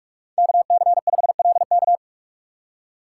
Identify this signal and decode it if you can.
Morse code, heard as a keyed tone, about 46 wpm, 700 Hz. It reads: KC5LK
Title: USA Call Signs - 2x Character Spacing - 45wpm